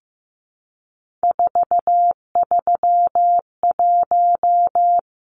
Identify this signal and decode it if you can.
Morse code, heard as a keyed tone, about 15 wpm, 700 Hz. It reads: 431
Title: Numbers - 3-Digit 15wpm